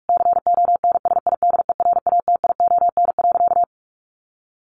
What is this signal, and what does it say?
Morse code, heard as a keyed tone, about 44 wpm, 700 Hz. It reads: CONSIDERATION.
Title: Sentences from Top 1500 Words 45wpm